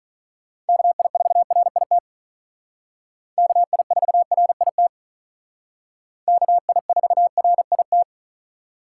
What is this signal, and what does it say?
Morse code, heard as a keyed tone, about 45 wpm, 700 Hz. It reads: KI4RIT KI4RIT KI4RIT